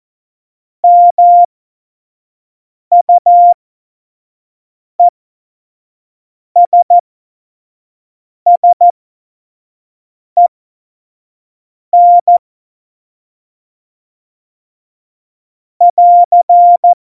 Morse code, heard as a keyed tone, about 14 wpm, spaced as for 5 wpm, 700 Hz. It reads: MUESSEN <AR>
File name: Morsekurs / Witze / Witz 20 March 2018, 25 F70